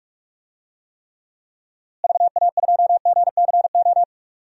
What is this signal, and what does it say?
Morse code, heard as a keyed tone, about 41 wpm, 700 Hz. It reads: VA2GCO
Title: DXCC-100 Call Signs - 6 Characters - Repeated 2x - 45wpm